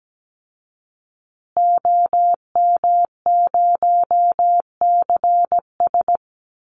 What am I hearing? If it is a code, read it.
Morse code, heard as a keyed tone, about 17 wpm, 700 Hz. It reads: OM0CS
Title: Call Signs Repeated 2x - 17wpm